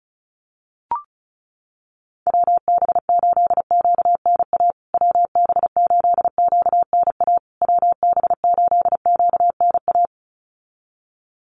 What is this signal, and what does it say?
Morse code, heard as a keyed tone, about 34 wpm, 700 Hz. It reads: WB8QNA WB8QNA WB8QNA
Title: USA Call Signs Repeated 3x 35wpm